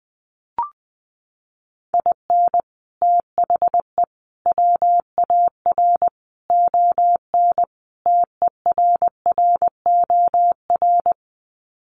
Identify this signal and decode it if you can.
Morse code, heard as a keyed tone, about 20 wpm, 700 Hz. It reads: IN THE WAR ON TERROR
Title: Sets of 5 Words Encore2 20wpm